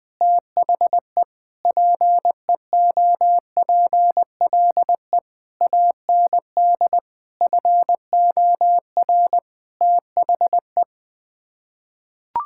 Morse code, heard as a keyed tone, about 20 wpm, 700 Hz. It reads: THE PEOPLE AND FOR THE
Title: Sets of 9 Words - 20wpm